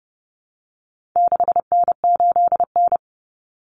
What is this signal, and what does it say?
Morse code, heard as a keyed tone, about 30 wpm, 700 Hz. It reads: 6N8N
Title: Common Cut Numbers - 4-Digit 30wpm